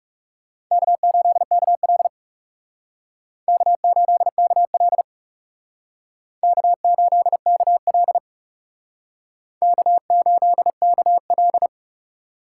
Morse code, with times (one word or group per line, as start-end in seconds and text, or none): 0.7-2.1 K8KL
3.5-5.0 K8KL
6.4-8.2 K8KL
9.6-11.7 K8KL